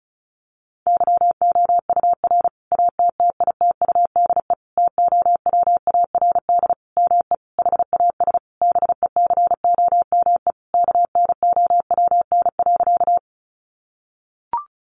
Morse code, heard as a keyed tone, about 34 wpm, 700 Hz. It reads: YOUR ATTITUDE TOWARD ME HAS BECOME KNOWN.
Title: Sentences from Top 1300 Words 35wpm